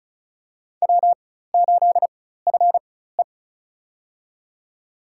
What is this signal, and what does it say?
Morse code, heard as a keyed tone, about 36 wpm, spaced as for 16 wpm, 700 Hz. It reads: W8FE
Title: USA Call Signs - 4 Characters - 4x Character Spacing - 35wpm